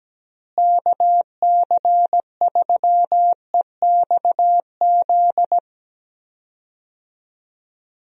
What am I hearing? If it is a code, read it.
Morse code, heard as a keyed tone, about 17 wpm, 700 Hz. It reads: KC3EXZ
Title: USA Call Signs - 6 Characters - Repeated 3x - 17wpm